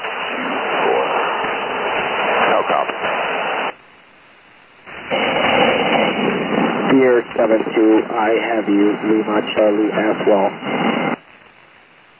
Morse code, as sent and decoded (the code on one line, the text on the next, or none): none